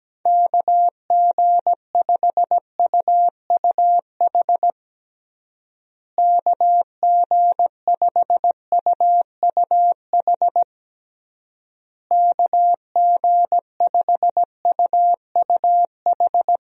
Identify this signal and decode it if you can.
Morse code, heard as a keyed tone, about 17 wpm, 700 Hz. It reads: KG5UUH KG5UUH KG5UUH
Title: USA Call Signs - 6 Characters - Repeated 3x - 17wpm